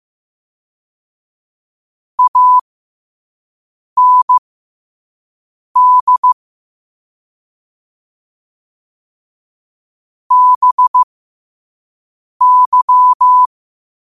Morse code, heard as a keyed tone, about 15 wpm, spaced as for 5 wpm, 1000 Hz. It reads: AND BY
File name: qotdfiles / 5wpm